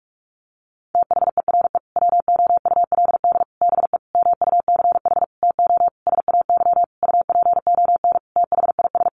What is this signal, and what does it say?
Morse code, heard as a keyed tone, about 44 wpm, 700 Hz. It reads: THERE WOULD BE MUCH TO SAY UPON THIS
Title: Sentences from Top 1200 Words 45wpm